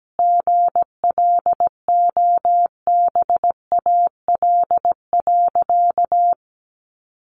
Morse code, heard as a keyed tone, about 17 wpm, 700 Hz. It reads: GLOBAL.
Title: Sentences from Top 1200 Words 17wpm